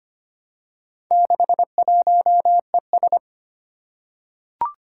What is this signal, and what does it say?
Morse code, heard as a keyed tone, about 25 wpm, 700 Hz. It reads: 61ES
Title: Four Challenging Letter-Number Combinations 25wpm